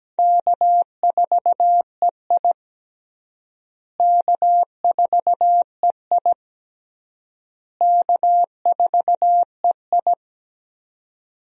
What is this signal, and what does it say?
Morse code, heard as a keyed tone, about 17 wpm, 700 Hz. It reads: K4EI K4EI K4EI
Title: USA Call Signs - 4 Characters - Repeated 3x - 17wpm